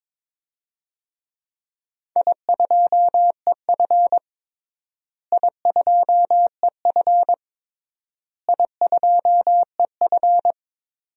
Morse code, heard as a keyed tone, about 22 wpm, 700 Hz. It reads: I2EF I2EF I2EF
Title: DXCC-100 Call Signs - 4 Characters - Repeated 3x - 22wpm